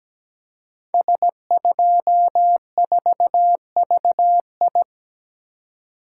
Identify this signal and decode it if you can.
Morse code, heard as a keyed tone, about 17 wpm, 700 Hz. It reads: S24VI
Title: Code Groups - 5 Letter:Numbers - 17wpm